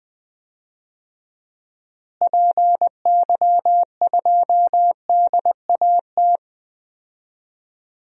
Morse code, heard as a keyed tone, about 21 wpm, 700 Hz. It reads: PY2DAT
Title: DXCC-100 Call Signs - 6 Characters - Repeated 3x - 20wpm